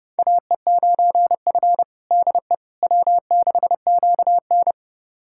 Morse code, heard as a keyed tone, about 28 wpm, 700 Hz. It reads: AE9F DE W6QN
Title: USA Callsign DE Callsign - 4 Characters - 30wpm